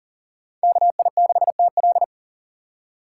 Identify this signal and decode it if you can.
Morse code, heard as a keyed tone, about 40 wpm, 700 Hz. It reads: KI6TL
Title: USA Call Signs - 5 Characters - 4x Character Spacing - 40wpm